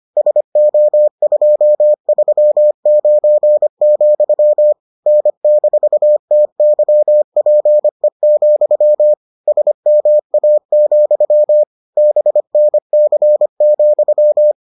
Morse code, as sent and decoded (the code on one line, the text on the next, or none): ... --- ..--- ...-- ----. --..-- / -. -....- - -.-- .--. --..-- / ... -- .- --..-- / -... -. -.-. --..--
SO239, N-TYPE, SMA, BNC,